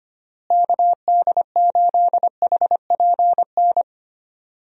Morse code, heard as a keyed tone, about 25 wpm, 700 Hz. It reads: KD8HPN
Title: USA Call Signs - 6 Characters - Repeated 3x - 25wpm